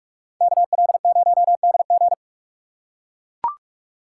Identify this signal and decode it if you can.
Morse code, heard as a keyed tone, about 40 wpm, 700 Hz. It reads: KL0DG